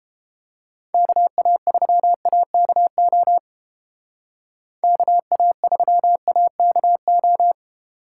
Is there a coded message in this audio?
Yes: Morse code from 0.9 to 7.5 s, heard as a keyed tone, about 31 wpm, 700 Hz. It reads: KA3AKO KA3AKO